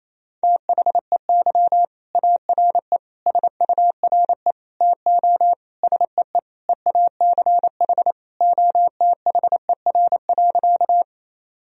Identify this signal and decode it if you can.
Morse code, heard as a keyed tone, about 27 wpm, 700 Hz. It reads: THEY ARE SURE TO SEE EACH OTHER.
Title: Sentences from Top 1000 Words - Review Part 1 - 28wpm